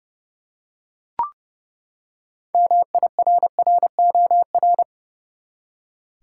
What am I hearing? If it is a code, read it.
Morse code, heard as a keyed tone, about 31 wpm, 700 Hz. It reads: MIRROR
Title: Six Letter Words - 30wpm